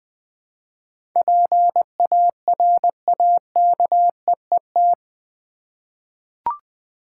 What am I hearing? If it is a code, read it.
Morse code, heard as a keyed tone, about 20 wpm, 700 Hz. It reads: PARAKEET
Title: Difficult Words with Hints - 20wpm